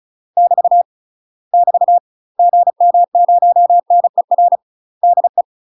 Morse code, heard as a keyed tone, about 34 wpm, 700 Hz. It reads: <BT> <BT> GM0NER DE
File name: UK MASTER QSOs TRAINING / 35WPM / UK MASTER QSOs 0037